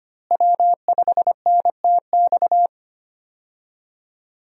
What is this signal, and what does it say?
Morse code, heard as a keyed tone, about 25 wpm, 700 Hz. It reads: W5NTX
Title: USA Call Signs - 5 Characters - Repeated 2x - 25wpm